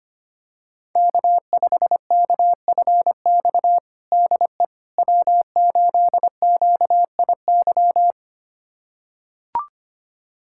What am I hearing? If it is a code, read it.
Morse code, heard as a keyed tone, about 25 wpm, 700 Hz. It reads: K5KFX DE W8QIY
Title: USA Callsign DE Callsign - 5 Characters - 25wpm